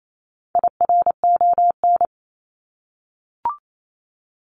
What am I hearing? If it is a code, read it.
Morse code, heard as a keyed tone, about 28 wpm, 700 Hz. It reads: IRON
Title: Top 1751-2000 Words 28wpm